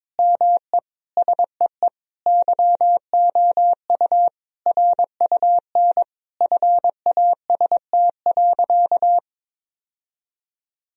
Morse code, heard as a keyed tone, about 22 wpm, 700 Hz. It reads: ME SEE YOU RUN FAST.
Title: Sentences from Top 1000 Words - Review Part 2 - 22wpm